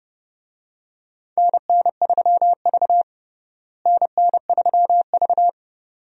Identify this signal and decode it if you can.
Morse code, heard as a keyed tone, about 30 wpm, 700 Hz. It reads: NN3V NN3V
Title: USA Call Signs - 4 Characters - Repeated 2x - 30wpm